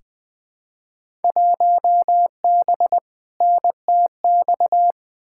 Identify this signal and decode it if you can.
Morse code, heard as a keyed tone, about 20 wpm, 700 Hz. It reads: 1B NTX
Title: Field Day Exchanges - 20wpm